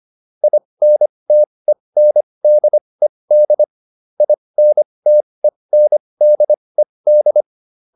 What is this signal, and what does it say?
Morse code, heard as a keyed tone, about 25 wpm, spaced as for 20 wpm, 600 Hz. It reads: INTENDED INTENDED
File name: IN8 20